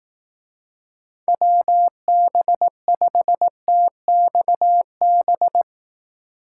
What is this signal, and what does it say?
Morse code, heard as a keyed tone, about 18 wpm, 700 Hz. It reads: WB5TXB
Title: Speed Racing - USA 2x3 Call Signs 15wpm